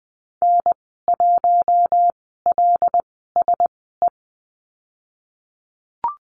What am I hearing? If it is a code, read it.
Morse code, heard as a keyed tone, about 20 wpm, spaced as for 15 wpm, 700 Hz. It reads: N1LSE
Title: 076 Lesson 13g - Letter L - Call Signs 20wpm